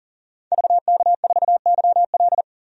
Morse code, heard as a keyed tone, about 40 wpm, 700 Hz. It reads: VK4YL
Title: DXCC-100 Call Signs - 5 Characters - 40wpm